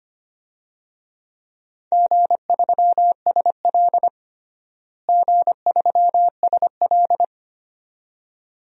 Morse code, heard as a keyed tone, about 25 wpm, 700 Hz. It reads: G3SL G3SL
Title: DXCC-100 Call Signs - 4 Characters - Repeated 2x - 25wpm